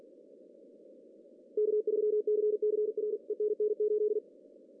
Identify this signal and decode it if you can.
Morse code, heard as a keyed tone, about 45 wpm, 420 Hz. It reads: K3CCR ENN 8
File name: CW / cqww-cw-k5tr